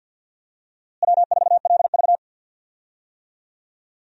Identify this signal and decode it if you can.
Morse code, heard as a keyed tone, about 52 wpm, 700 Hz. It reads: W4LV
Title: USA Call Signs - 4 Characters - Repeated 2x - 50wpm